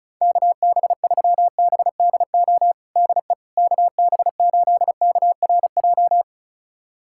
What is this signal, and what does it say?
Morse code, heard as a keyed tone, about 36 wpm, 700 Hz. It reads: KB3BDO DE KB8KRJ
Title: USA Callsign DE Callsign 35wpm